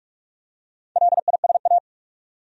Morse code, heard as a keyed tone, about 47 wpm, 700 Hz. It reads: LISA
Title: USA Common Names - Max 4 Letters - 45wpm